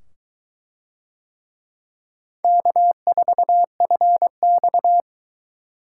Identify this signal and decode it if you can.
Morse code, heard as a keyed tone, about 22 wpm, 700 Hz. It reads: K4FX